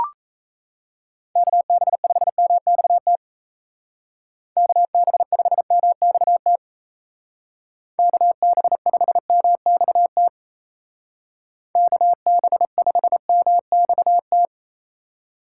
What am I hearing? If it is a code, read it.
Morse code, heard as a keyed tone, about 34 wpm, 700 Hz. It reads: KB5MXT KB5MXT KB5MXT KB5MXT
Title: Speed Racing - USA Call Signs 28wpm